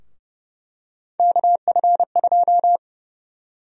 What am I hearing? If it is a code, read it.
Morse code, heard as a keyed tone, about 31 wpm, 700 Hz. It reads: KF2
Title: USA Call Sign Prefixes - Repeated 2x - 30wpm